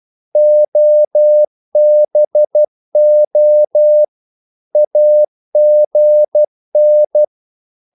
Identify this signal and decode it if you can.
Morse code, heard as a keyed tone, about 12 wpm, 600 Hz. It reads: OBO AGN
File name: Korte ord DK 12wpm